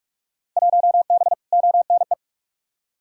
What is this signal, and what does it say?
Morse code, heard as a keyed tone, about 41 wpm, 700 Hz. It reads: 1B ONE
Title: Field Day Exchanges - 45wpm